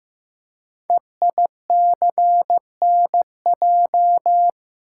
Morse code, heard as a keyed tone, about 15 wpm, 700 Hz. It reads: EICNJ
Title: Code Groups - 5 Letters - 15wpm